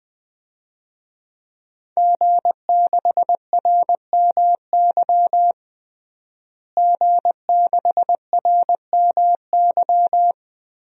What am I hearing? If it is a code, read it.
Morse code, heard as a keyed tone, about 20 wpm, 700 Hz. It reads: G6RMY G6RMY
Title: DXCC-100 Call Signs - 5 Characters - Repeated 2x - 20wpm